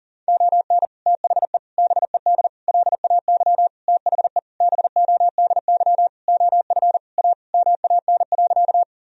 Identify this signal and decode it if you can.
Morse code, heard as a keyed tone, about 40 wpm, 700 Hz. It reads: ON THE BED LAY THE BODY OF A MAN.
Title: Sentences from Top 1000 Words - Review Part 2 - 40wpm